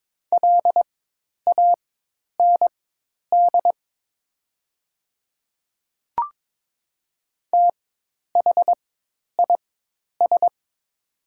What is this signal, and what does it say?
Morse code, heard as a keyed tone, about 23 wpm, spaced as for 10 wpm, 690 Hz. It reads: LAND THIS
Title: 4x Character Spacing - Top 200 Words - 22wpm